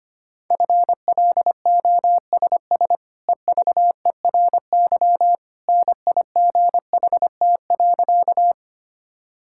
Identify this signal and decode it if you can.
Morse code, heard as a keyed tone, about 25 wpm, 690 Hz. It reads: FLOSS EVERY NIGHT.
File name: Easy Sentences with Hints - 25wpm